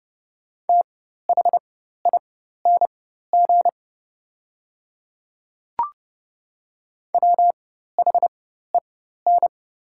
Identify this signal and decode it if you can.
Morse code, heard as a keyed tone, about 30 wpm, spaced as for 14 wpm, 700 Hz. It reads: THING WHEN
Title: 4x Character Spacing - Top 300 Words - 30wpm